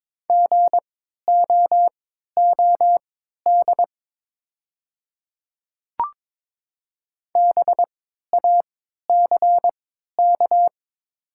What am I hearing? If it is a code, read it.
Morse code, heard as a keyed tone, about 22 wpm, spaced as for 13 wpm, 700 Hz. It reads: GOOD BACK
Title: 3x Character Spacing - Top 200 Words - 22wpm